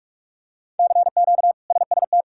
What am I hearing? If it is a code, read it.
Morse code, heard as a keyed tone, about 41 wpm, 700 Hz. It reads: CQ SST
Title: K1USN SST QSO Practice - 45wpm